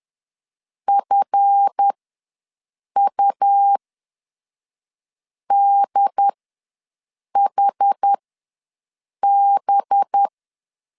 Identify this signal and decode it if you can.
Morse code, heard as a keyed tone, about 10 wpm, spaced as for 5 wpm, 800 Hz. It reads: FUDHB
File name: LEZIONI / 8 YJ45 / 8 YJ ws 10wpm 4043